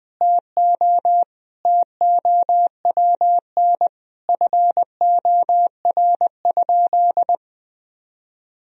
Morse code, heard as a keyed tone, about 20 wpm, 700 Hz. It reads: TO TOWN FOR?
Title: Sentences from Top 1000 Words - Review Part 2 - 20wpm